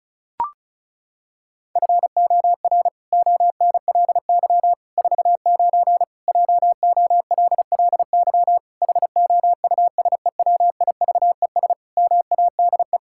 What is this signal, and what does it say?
Morse code, heard as a keyed tone, about 36 wpm, 700 Hz. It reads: FOR ONLY 49 JOLLY HOUSEWIVES MADE
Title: Pangram Challenge v2 - 35wpm